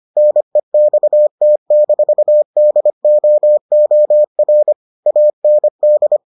Morse code, heard as a keyed tone, about 25 wpm, 600 Hz. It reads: NEXT-DOOR AND